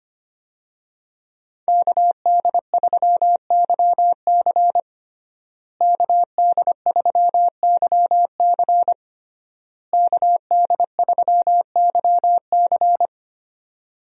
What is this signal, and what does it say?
Morse code, heard as a keyed tone, about 25 wpm, 700 Hz. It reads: KD3YC KD3YC KD3YC
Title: USA Call Signs - 5 Characters - Repeated 3x - 25wpm